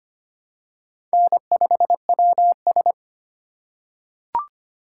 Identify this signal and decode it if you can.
Morse code, heard as a keyed tone, about 25 wpm, 700 Hz. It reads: N5WS
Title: USA Call Signs - 4 Characters - 4x Character Spacing - 25wpm